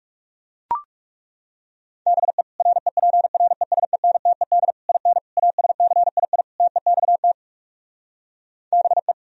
Morse code, heard as a keyed tone, about 40 wpm, 700 Hz. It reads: BE REPRESENTED IN ASCII TEXT BE